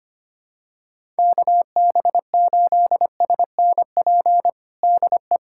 Morse code, heard as a keyed tone, about 25 wpm, 700 Hz. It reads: KB8SNP DE
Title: USA Callsign DE Callsign - 6 Characters - 25wpm